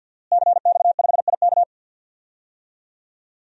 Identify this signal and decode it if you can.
Morse code, heard as a keyed tone, about 52 wpm, 700 Hz. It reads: CX5IK
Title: DXCC-100 Call Signs - 5 Characters - Repeated 2x - 50wpm